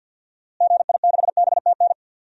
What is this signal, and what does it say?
Morse code, heard as a keyed tone, about 52 wpm, 700 Hz. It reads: GI6BTN
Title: DXCC-100 Call Signs - 6 Characters - Repeated 3x - 50wpm